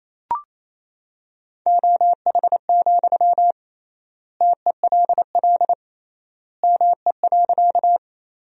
Morse code, heard as a keyed tone, about 28 wpm, 700 Hz. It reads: OH, TELL ME.
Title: 3x Word-Spacing - Sentences from Top 500 Words - 28wpm